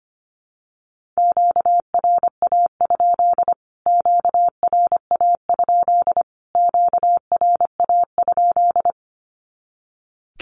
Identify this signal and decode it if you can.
Morse code, heard as a keyed tone, about 25 wpm, 700 Hz. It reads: QRA? QRA? QRA?
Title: Q-Codes Repeated 3x 25wpm